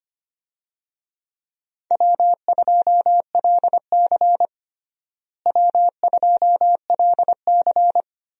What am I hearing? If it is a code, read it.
Morse code, heard as a keyed tone, about 25 wpm, 700 Hz. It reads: W2LC W2LC
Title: USA Call Signs - 4 Characters - Repeated 2x - 25wpm